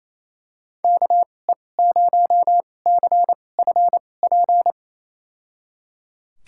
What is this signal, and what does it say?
Morse code, heard as a keyed tone, about 28 wpm, spaced as for 20 wpm, 700 Hz. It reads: KE0CFP
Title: USA Call Signs - 2x Character Spacing - 28wpm